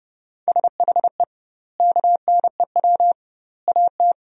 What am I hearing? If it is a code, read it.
Morse code, heard as a keyed tone, about 30 wpm, 700 Hz. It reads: SHE KNEW AT
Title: 2x Word-Spacing - Sentences from Top 400 Words - 30wpm